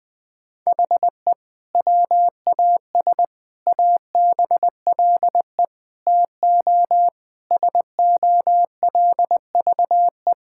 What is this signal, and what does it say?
Morse code, heard as a keyed tone, about 20 wpm, 700 Hz. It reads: HE WAS ABLE TO SOLVE
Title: Sentences with 3-Word Phrases - 20wpm